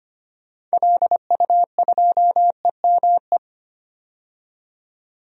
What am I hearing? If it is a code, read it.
Morse code, heard as a keyed tone, about 25 wpm, 690 Hz. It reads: LU2EME
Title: DXCC-100 Call Signs - 6 Characters - 25wpm